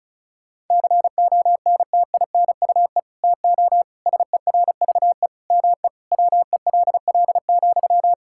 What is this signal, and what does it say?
Morse code, heard as a keyed tone, about 36 wpm, 700 Hz. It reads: CONTINUE TO SERVE ME WELL,
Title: Sentences from Top 1000 Words - Review Part 6 - 35wpm